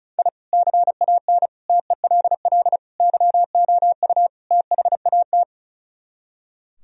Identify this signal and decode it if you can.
Morse code, heard as a keyed tone, about 42 wpm, 700 Hz. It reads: I CAN TELL YOU THAT
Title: Speed-Racing - Sets of 5 Words 35wpm